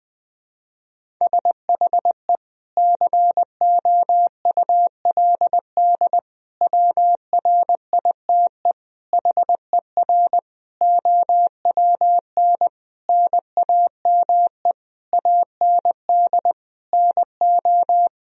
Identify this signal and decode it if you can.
Morse code, heard as a keyed tone, about 20 wpm, 700 Hz. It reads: SHE COULD WRITE HER OWN NAME AND NO
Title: Sentences from Top 1000 Words - Review Part 1 - 20wpm